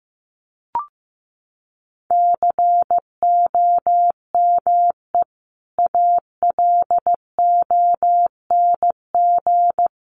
Morse code, heard as a keyed tone, about 15 wpm, 700 Hz. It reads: COME ALONG
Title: Sentences from Top 700 Words 15wpm